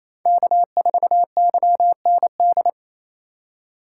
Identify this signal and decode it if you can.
Morse code, heard as a keyed tone, about 28 wpm, 700 Hz. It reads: K4YND
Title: K1USN SST QSO Practice - 28wpm